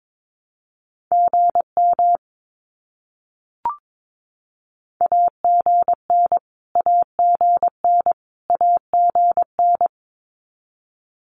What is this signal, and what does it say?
Morse code, heard as a keyed tone, about 22 wpm, 700 Hz. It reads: GM AGN AGN AGN
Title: Abbreviations Repeated 3x - 22wpm